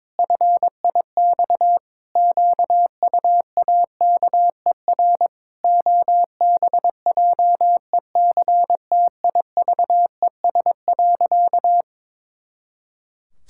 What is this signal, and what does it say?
Morse code, heard as a keyed tone, about 22 wpm, 700 Hz. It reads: FIX QUAKER OBJECTIVES.
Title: Pangram Challenge v2 - 22wpm